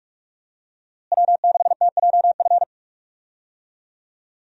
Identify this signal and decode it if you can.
Morse code, heard as a keyed tone, about 46 wpm, 700 Hz. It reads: W6TJF
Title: USA Call Signs - 5 Characters - Repeated 3x - 45wpm